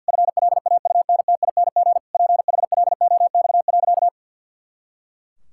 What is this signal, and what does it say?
Morse code, heard as a keyed tone, about 51 wpm, 700 Hz. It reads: FLAUNTING PHLOX.
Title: Pangram Challenge v2 - 50wpm